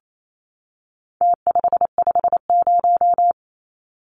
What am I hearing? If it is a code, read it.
Morse code, heard as a keyed tone, about 28 wpm, 700 Hz. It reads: T550
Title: Numbers - Contest Style - 4-Digit 28wpm